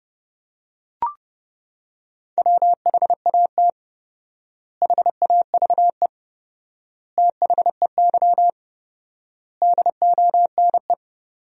Morse code, heard as a keyed tone, about 31 wpm, 700 Hz. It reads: WHAT HAVE THEY DONE
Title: 4x Word-Spacing - Sentences from Top 500 Words - 30wpm